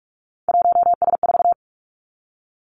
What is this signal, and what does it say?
Morse code, heard as a keyed tone, about 46 wpm, 700 Hz. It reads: 1S4
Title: Three Challenging Letter-Number Combinations 45wpm